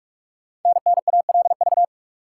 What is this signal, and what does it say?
Morse code, heard as a keyed tone, about 44 wpm, 700 Hz. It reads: NNALV
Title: Code Groups - 5 Letter:Numbers - 45wpm